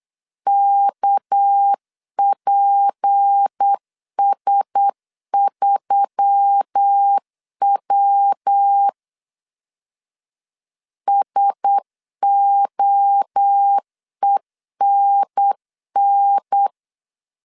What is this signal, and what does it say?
Morse code, heard as a keyed tone, about 8 wpm, 800 Hz. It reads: KPS3W SOENN
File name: CW / DET / 1 6 1500C S B 8wpm 5142